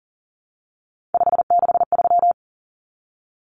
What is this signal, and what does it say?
Morse code, heard as a keyed tone, about 40 wpm, 700 Hz. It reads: 563
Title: Numbers - 3-Digit 40wpm